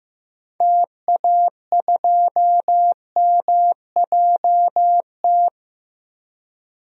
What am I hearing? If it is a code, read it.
Morse code, heard as a keyed tone, about 15 wpm, 700 Hz. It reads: TA2MJT
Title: DXCC-100 Call Signs - 6 Characters - 15wpm